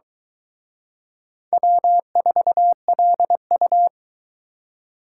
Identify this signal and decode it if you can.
Morse code, heard as a keyed tone, about 23 wpm, 700 Hz. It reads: W4LU